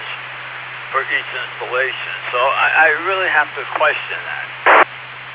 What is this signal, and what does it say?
Emergency Alert System (SAME) tones, heard as harsh buzzing bursts, with speech. It is heard from 1.1 to 4.6 s.